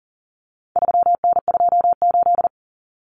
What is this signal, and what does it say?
Morse code, heard as a keyed tone, about 40 wpm, 700 Hz. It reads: 3N28
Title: Common Cut Numbers - 4-Digit 40wpm